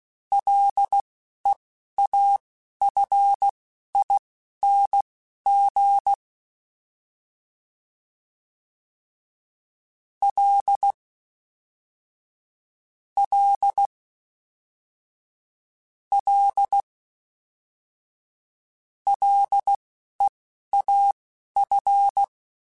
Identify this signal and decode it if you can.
Morse code, heard as a keyed tone, about 16 wpm, spaced as for 11 wpm, 800 Hz. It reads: LEAFING L L L LEAF